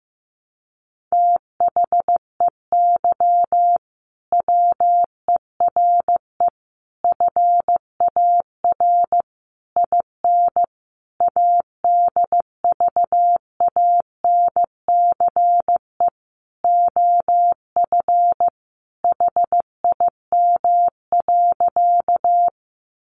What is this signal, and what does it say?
Morse code, heard as a keyed tone, about 15 wpm, 700 Hz. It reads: THEY WERE FAR IN ADVANCE OF HIM.
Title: Sentences from Top 1400 Words 15wpm